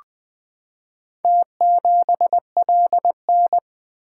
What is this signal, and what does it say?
Morse code, heard as a keyed tone, about 20 wpm, 700 Hz. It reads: T7LN
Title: Four Letter-Number Combinations - 20wpm